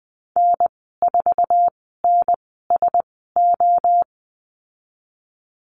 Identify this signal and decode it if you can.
Morse code, heard as a keyed tone, about 20 wpm, spaced as for 14 wpm, 700 Hz. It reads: N4NSO
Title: 082 Lesson 14g - Number 2 - Call Signs 20wpm